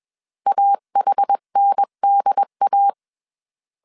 Morse code, heard as a keyed tone, about 21 wpm, 800 Hz. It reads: AHNDA